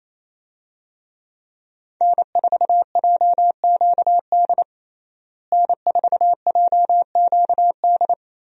Morse code, heard as a keyed tone, about 28 wpm, 700 Hz. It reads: N4JQD N4JQD
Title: USA Call Signs - 5 Characters - Repeated 3x - 28wpm